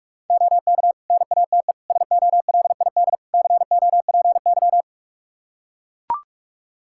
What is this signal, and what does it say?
Morse code, heard as a keyed tone, about 44 wpm, 700 Hz. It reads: OK NATE SOLID COPY
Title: Practice QSO - Exchange 3 - More Info - 45wpm